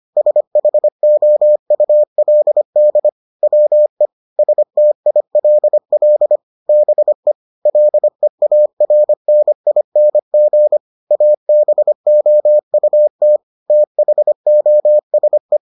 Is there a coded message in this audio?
Yes: Morse code from 0.2 to 15.6 s, heard as a keyed tone, about 25 wpm, 600 Hz. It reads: SHOULD WE STILL BE LEARNING ABOUT THOSE